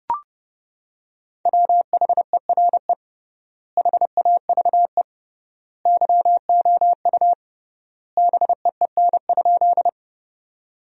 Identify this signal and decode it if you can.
Morse code, heard as a keyed tone, about 32 wpm, 700 Hz. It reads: WHERE HAVE YOU BEEN?
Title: 3x Word-Spacing - Sentences from Top 200 Words - 30wpm